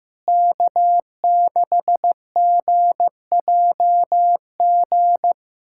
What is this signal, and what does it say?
Morse code, heard as a keyed tone, about 16 wpm, 700 Hz. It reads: K6GJG
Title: USA Callsign DE Callsign - 5 Characters - 15wpm